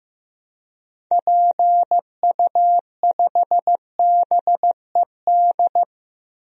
Morse code, heard as a keyed tone, about 15 wpm, 700 Hz. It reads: PU5BED
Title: DXCC-100 Call Signs - 6 Characters - 15wpm